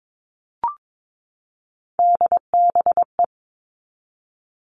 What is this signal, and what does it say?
Morse code, heard as a keyed tone, about 22 wpm, 700 Hz. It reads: DBE
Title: All Cut Numbers - 3-Digit 22wpm